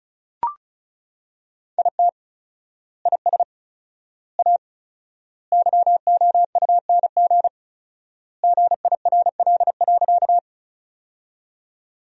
Morse code, heard as a keyed tone, about 36 wpm, 700 Hz. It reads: IT IS A YOUNG GIRL.
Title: 4x Word-Spacing - Sentences from Top 400 Words - 35wpm